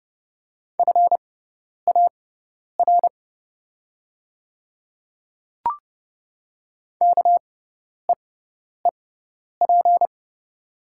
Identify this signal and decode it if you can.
Morse code, heard as a keyed tone, about 30 wpm, spaced as for 10 wpm, 700 Hz. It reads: FAR KEEP
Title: 6x Character Spacing - Top 500 Words - 30wpm